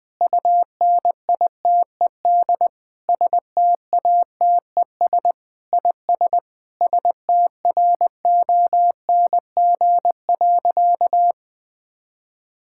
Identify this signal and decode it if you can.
Morse code, heard as a keyed tone, about 21 wpm, 700 Hz. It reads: UNITED STATES IS STRONG.
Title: Sentences with 8-Word Phrases - 20wpm